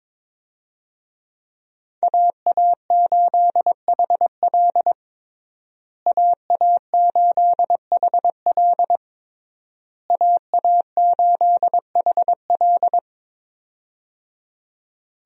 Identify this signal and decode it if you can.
Morse code, heard as a keyed tone, about 22 wpm, 700 Hz. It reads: AA8HL AA8HL AA8HL
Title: USA Call Signs - 5 Characters - Repeated 3x - 22wpm